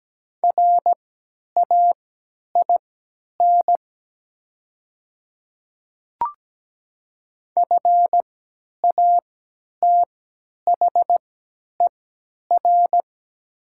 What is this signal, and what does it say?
Morse code, heard as a keyed tone, about 17 wpm, spaced as for 10 wpm, 700 Hz. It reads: RAIN FATHER
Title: 3x Character Spacing - Top 300 Words - 17wpm